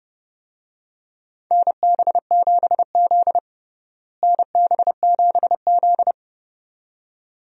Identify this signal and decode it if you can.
Morse code, heard as a keyed tone, about 30 wpm, 700 Hz. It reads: NB7Z NB7Z
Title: USA Call Signs - 4 Characters - Repeated 2x - 30wpm